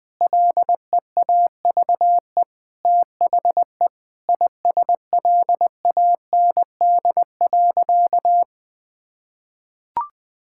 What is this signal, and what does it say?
Morse code, heard as a keyed tone, about 19 wpm, 700 Hz. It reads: LEAVE THE ISLAND.
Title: Sentences from Top 1000 Words - Review Part 6 - 20wpm